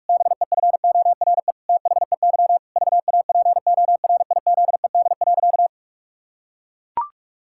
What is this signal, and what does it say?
Morse code, heard as a keyed tone, about 41 wpm, 700 Hz. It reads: BEFORE THEY VAPORIZED.
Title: Pangram Challenge v2 - 45wpm